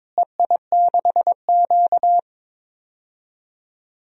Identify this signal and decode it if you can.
Morse code, heard as a keyed tone, about 22 wpm, 700 Hz. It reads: EI6Q
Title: DXCC-100 Call Signs - 4 Characters - Repeated 3x - 22wpm